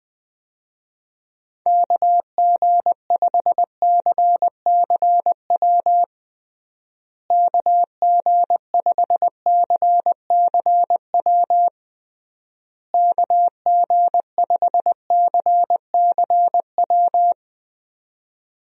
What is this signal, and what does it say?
Morse code, heard as a keyed tone, about 20 wpm, 700 Hz. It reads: KG5CCW KG5CCW KG5CCW
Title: USA Call Signs - 6 Characters - Repeated 3x - 20wpm